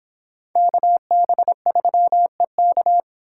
Morse code, heard as a keyed tone, about 22 wpm, 700 Hz. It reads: KB3EK